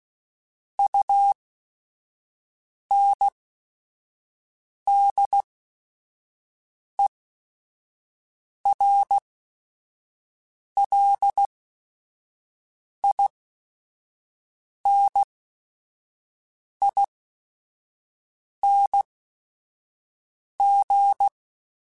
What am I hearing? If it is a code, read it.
Morse code, heard as a keyed tone, about 16 wpm, spaced as for 5 wpm, 800 Hz. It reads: UNDERLINING